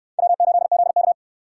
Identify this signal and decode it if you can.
Morse code, heard as a keyed tone, about 69 wpm, 700 Hz. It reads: W8ZK